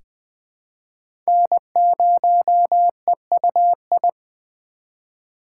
Morse code, heard as a keyed tone, about 20 wpm, 700 Hz. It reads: N0EUI
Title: USA Call Signs - 5 Characters - 2x Character Spacing - 20wpm